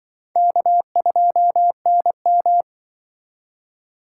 Morse code, heard as a keyed tone, about 24 wpm, 700 Hz. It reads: K2NM
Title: Speed Racing - USA 1x2 Call Signs 20wpm